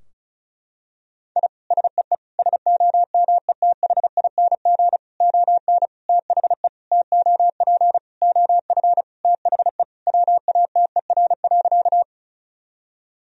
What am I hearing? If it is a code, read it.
Morse code, heard as a keyed tone, about 34 wpm, 700 Hz. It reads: I SEE SOMETHING ON THE TOP OF THE WATER.
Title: Sentences from Top 1000 Words - Review Part 2 - 35wpm